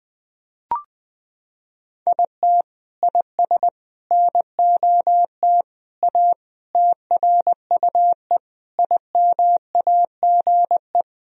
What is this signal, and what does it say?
Morse code, heard as a keyed tone, about 20 wpm, 700 Hz. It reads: IT IS NOT A TRUE IMAGE
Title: Sentences from Top 1000 Words - Review Part 4 - 20wpm